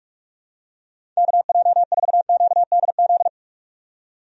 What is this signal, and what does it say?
Morse code, heard as a keyed tone, about 46 wpm, 700 Hz. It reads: KJ4QDZ
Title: USA Call Signs - 6 Characters - 45wpm